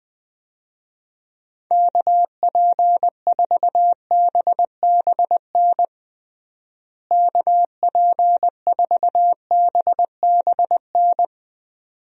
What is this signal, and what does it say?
Morse code, heard as a keyed tone, about 21 wpm, 700 Hz. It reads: KP4BBN KP4BBN
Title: DXCC-100 Call Signs - 6 Characters - Repeated 2x - 20wpm